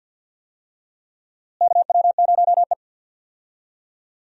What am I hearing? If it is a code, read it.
Morse code, heard as a keyed tone, about 47 wpm, 700 Hz. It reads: KW0E
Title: USA Call Signs - 4 Characters - Repeated 3x - 50wpm